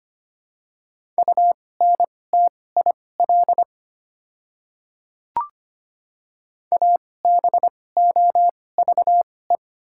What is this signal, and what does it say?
Morse code, heard as a keyed tone, about 25 wpm, spaced as for 18 wpm, 700 Hz. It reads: UNTIL ABOVE
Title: 2x Character Spacing - Top 500 Words - 25wpm